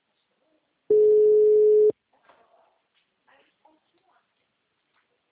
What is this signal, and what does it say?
Ringback tone (Europe, 425 Hz), with speech.